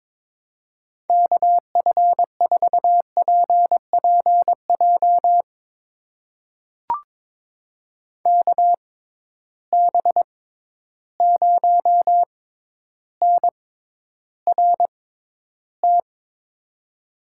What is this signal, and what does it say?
Morse code, heard as a keyed tone, about 23 wpm, spaced as for 11 wpm, 700 Hz. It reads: KF4PPJ KB0NRT
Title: USA Call Signs - 6 Characters - 6x Character Spacing - 22wpm